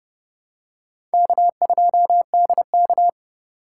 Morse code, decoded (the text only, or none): K2DK